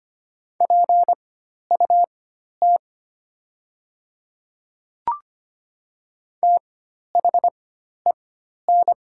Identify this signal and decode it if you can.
Morse code, heard as a keyed tone, about 25 wpm, spaced as for 12 wpm, 700 Hz. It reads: PUT THEN
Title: 4x Character Spacing - Top 300 Words - 25wpm